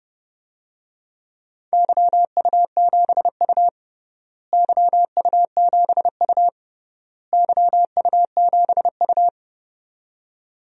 Morse code, heard as a keyed tone, about 32 wpm, 700 Hz. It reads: YU7U YU7U YU7U
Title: DXCC-100 Call Signs - 4 Characters - Repeated 3x - 30wpm